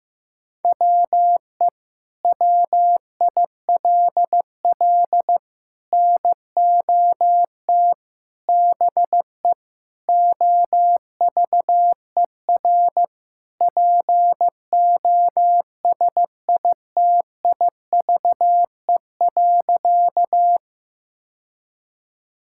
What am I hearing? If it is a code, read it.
Morse code, heard as a keyed tone, about 15 wpm, 700 Hz. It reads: WE WILL NOT BE OVER POSITIVE.
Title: Sentences from Top 1000 Words - Review Part 5 - 15wpm